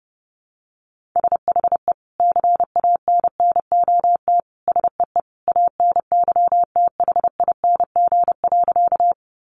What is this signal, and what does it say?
Morse code, heard as a keyed tone, about 30 wpm, 700 Hz. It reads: SHE CANNOT SEE ANYTHING.
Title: Additional Easy Phrases 30wpm